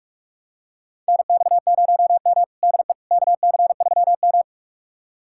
Morse code, heard as a keyed tone, about 42 wpm, 700 Hz. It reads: NX0M DE KC3M
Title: USA Callsign DE Callsign - 4 Characters - 45wpm